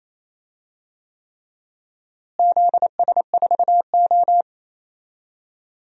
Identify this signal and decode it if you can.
Morse code, heard as a keyed tone, about 28 wpm, 700 Hz. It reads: ZS4O
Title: DXCC-100 Call Signs - 4 Characters - Repeated 3x - 28wpm